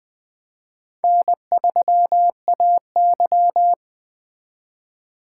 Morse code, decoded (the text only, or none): N3AY